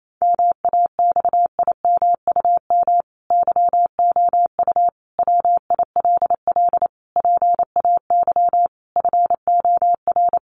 Morse code, heard as a keyed tone, about 28 wpm, 700 Hz. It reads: MAXIMUM YOU WILL PAY FOR
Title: Sentences from Top 1300 Words 28wpm